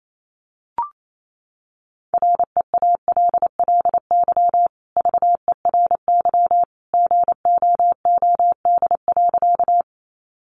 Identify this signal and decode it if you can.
Morse code, heard as a keyed tone, about 28 wpm, 700 Hz. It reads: REALLY VERY GOOD.
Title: Sentences from Top 600 Words 28wpm